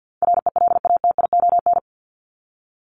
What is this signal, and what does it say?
Morse code, heard as a keyed tone, about 48 wpm, 700 Hz. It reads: RELATION
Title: ES:S Suffix - Top 100 Words with Base Words 50wpm